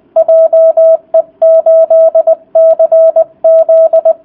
Morse code, heard as a keyed tone, about 20 wpm, 650 Hz. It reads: JE8CZ